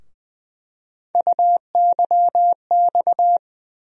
Morse code, heard as a keyed tone, about 20 wpm, 700 Hz. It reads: UYX
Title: USA Call Sign Postfixes - 3 Letters 20wpm